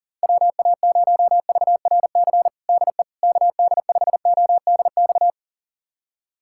Morse code, heard as a keyed tone, about 40 wpm, 700 Hz. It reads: WA0VRC DE KD5ODX
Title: USA Callsign DE Callsign - 6 Characters - 40wpm